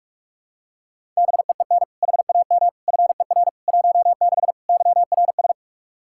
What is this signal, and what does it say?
Morse code, heard as a keyed tone, about 41 wpm, 700 Hz. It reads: BEEN HAM FER 16 YRS
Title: K1USN SST QSO Practice 45wpm